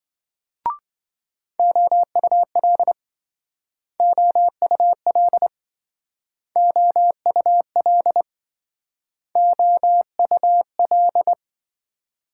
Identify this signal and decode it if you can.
Morse code, heard as a keyed tone, about 26 wpm, 700 Hz. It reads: OUL OUL OUL OUL